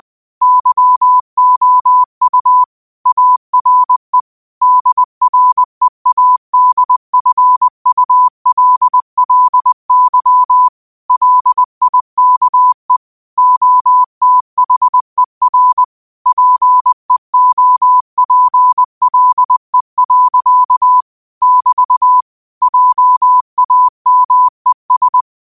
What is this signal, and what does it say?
Morse code, heard as a keyed tone, about 20 wpm, 1000 Hz. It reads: YOU ARE DREADFULLY LIKE OTHER PEOPLE. <BT> JAMES
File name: qotdfiles / 20wpm